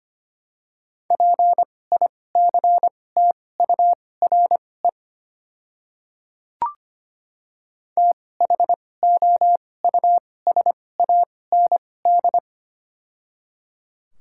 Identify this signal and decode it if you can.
Morse code, heard as a keyed tone, about 25 wpm, spaced as for 18 wpm, 700 Hz. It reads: PICTURE THOUSAND
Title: 2x Character Spacing - Top 500 Words - 25wpm